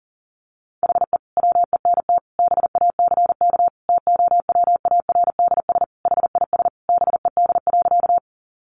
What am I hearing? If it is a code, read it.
Morse code, heard as a keyed tone, about 40 wpm, 700 Hz. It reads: HE WENT BACK TOWARDS HIS BED.
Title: Sentences from Top 1100 Words 40wpm